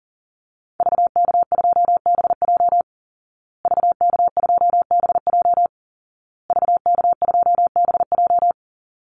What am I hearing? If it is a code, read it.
Morse code, heard as a keyed tone, about 40 wpm, 700 Hz. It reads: VK2BJ VK2BJ VK2BJ
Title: Call Signs Repeated 2x - 40wpm